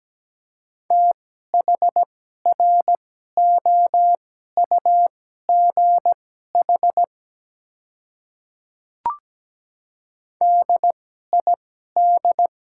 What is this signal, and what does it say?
Morse code, heard as a keyed tone, about 17 wpm, spaced as for 12 wpm, 700 Hz. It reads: THROUGH DID
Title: 2x Character Spacing - Top 200 Words - 17wpm